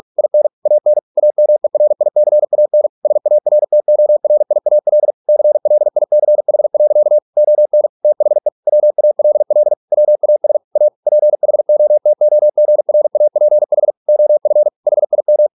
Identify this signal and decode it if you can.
Morse code, heard as a keyed tone, about 46 wpm, 600 Hz. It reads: IN AN AMERICAN SARTORIAL CLICH. ON THE WALL WAS A PHOTOGRAPH OF HIM